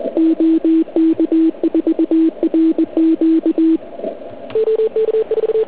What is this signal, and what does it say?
Morse code, heard as a keyed tone, about 20 wpm, 320 Hz. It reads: OK4RQ